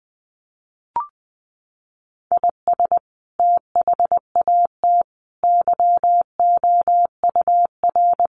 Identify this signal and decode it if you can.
Morse code, heard as a keyed tone, about 20 wpm, 700 Hz. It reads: IS THAT YOUR
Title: Sentences from Top 1100 Words 20wpm